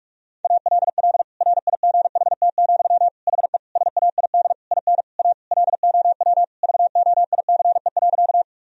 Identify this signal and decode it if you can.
Morse code, heard as a keyed tone, about 41 wpm, 700 Hz. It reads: ALL RIGHT, HE SAID IN A LOW VOICE.
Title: Sentences from Top 1000 Words - Review Part 1 - 45wpm